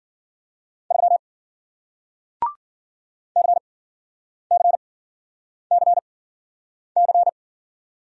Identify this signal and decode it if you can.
Morse code, heard as a keyed tone, about 53 wpm, spaced as for 9 wpm, 700 Hz. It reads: 3 CCCC